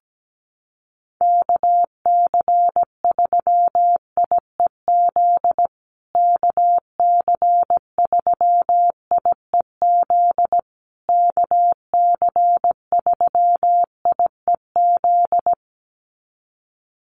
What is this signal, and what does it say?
Morse code, heard as a keyed tone, about 17 wpm, 700 Hz. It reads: KC3IEZ KC3IEZ KC3IEZ
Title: USA Call Signs Repeated 3x 17wpm